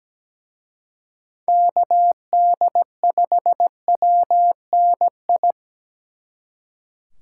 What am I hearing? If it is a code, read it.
Morse code, heard as a keyed tone, about 20 wpm, 700 Hz. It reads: KD5WNI
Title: Speed Racing - USA 2x3 Call Signs 17wpm